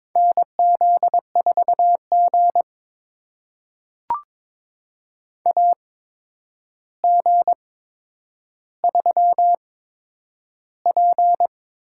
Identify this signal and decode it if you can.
Morse code, heard as a keyed tone, about 22 wpm, spaced as for 9 wpm, 700 Hz. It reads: NZ4G AG3P
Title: USA Call Signs - 4 Characters - 8x Character Spacing - 22wpm